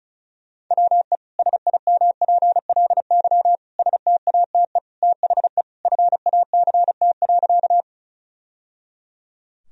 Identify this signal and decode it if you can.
Morse code, heard as a keyed tone, about 34 wpm, 700 Hz. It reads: WE SIMPLY STATE THE FACT.
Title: Sentences from Top 1000 Words - Review Part 4 - 35wpm